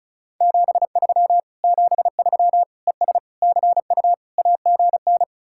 Morse code, heard as a keyed tone, about 46 wpm, 700 Hz. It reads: 73 73 ES CU AGN